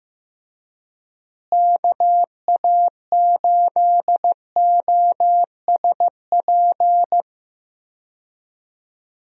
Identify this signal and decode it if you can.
Morse code, heard as a keyed tone, about 15 wpm, 700 Hz. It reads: KA8OSP
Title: USA Call Signs - 6 Characters - Repeated 2x - 15wpm